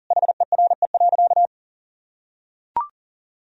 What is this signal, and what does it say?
Morse code, heard as a keyed tone, about 40 wpm, 700 Hz. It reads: HERE.
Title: Sentences from Top 1000 Words - Review Part 4 - 40wpm